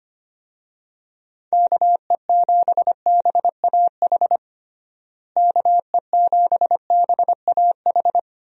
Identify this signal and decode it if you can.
Morse code, heard as a keyed tone, about 25 wpm, 700 Hz. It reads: KE7BAH KE7BAH
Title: USA Call Signs - 6 Characters - Repeated 2x - 25wpm